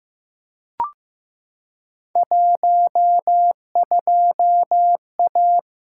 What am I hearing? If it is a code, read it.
Morse code, heard as a keyed tone, about 15 wpm, 700 Hz. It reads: 12A
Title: Field Day Exchanges Encore - 15wpm